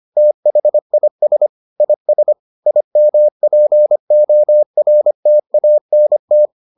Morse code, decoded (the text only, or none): THIS IS IMPORTANT